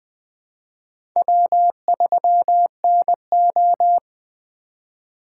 Morse code, heard as a keyed tone, about 20 wpm, 700 Hz. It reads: W3NO